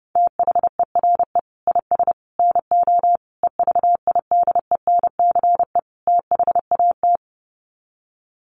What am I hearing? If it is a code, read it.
Morse code, heard as a keyed tone, about 30 wpm, 700 Hz. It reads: THERE IS NO EVIDENCE THAT
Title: Sets of 5 Words Encore 30wpm